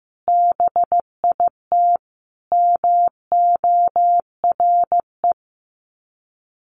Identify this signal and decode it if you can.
Morse code, heard as a keyed tone, about 15 wpm, 700 Hz. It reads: BIT MORE
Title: Sets of 4 Words 15wpm